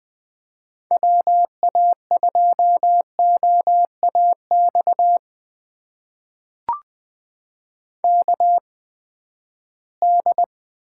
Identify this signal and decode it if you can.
Morse code, heard as a keyed tone, about 21 wpm, spaced as for 14 wpm, 700 Hz. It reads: WA2OAX KD
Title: USA Call Signs - 6 Characters - 8x Character Spacing - 20wpm